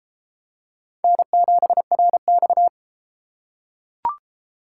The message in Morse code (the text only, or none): N7RX